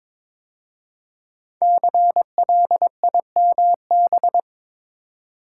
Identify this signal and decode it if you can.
Morse code, heard as a keyed tone, about 22 wpm, 700 Hz. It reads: CLIMB
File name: 5 Letter Words - First 3 Letters Given - 22wpm